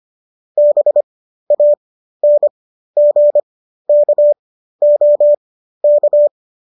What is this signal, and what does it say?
Morse code, heard as a keyed tone, about 25 wpm, spaced as for 13 wpm, 590 Hz. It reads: BANGKOK